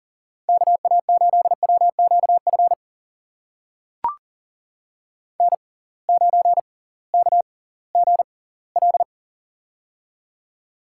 Morse code, heard as a keyed tone, about 39 wpm, spaced as for 21 wpm, 700 Hz. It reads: KA8WQF N9KGL
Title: USA Call Signs - 6x Character Spacing - 40wpm